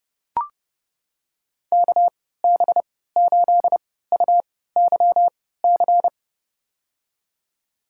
Morse code, heard as a keyed tone, about 32 wpm, spaced as for 17 wpm, 700 Hz. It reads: KB8UYC
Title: USA Call Signs - 6 Characters - 3x Character Spacing - 30wpm